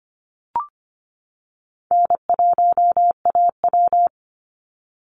Morse code, heard as a keyed tone, about 25 wpm, 700 Hz. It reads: N1AW
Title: NA SOTA Chaser Callsigns 25wpm